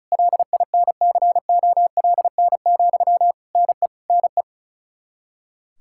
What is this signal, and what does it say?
Morse code, heard as a keyed tone, about 34 wpm, 700 Hz. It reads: LINCOLN, NE NE
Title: Practice QSO - Exchange 1 - Beginning - 35wpm